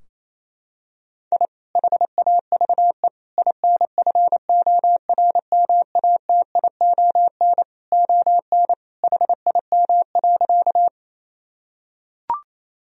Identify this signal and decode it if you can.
Morse code, heard as a keyed tone, about 27 wpm, 700 Hz. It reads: I HAVE INFORMATION ON HIM.
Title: Sentences from Top 1000 Words - Review Part 3 - 28wpm